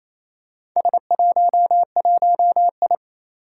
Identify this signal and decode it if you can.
Morse code, heard as a keyed tone, about 28 wpm, 700 Hz. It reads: S11I
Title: Four Challenging Letter-Number Combinations 28wpm